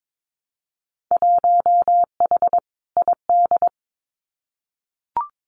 Morse code, heard as a keyed tone, about 22 wpm, 700 Hz. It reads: 1H ID
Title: Winter Field Day Exchanges 22wpm